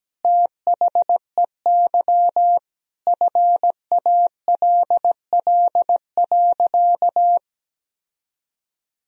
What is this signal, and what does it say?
Morse code, heard as a keyed tone, about 17 wpm, 700 Hz. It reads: THEY FALL.
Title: Sentences from Top 1000 Words - Review Part 2 - 17wpm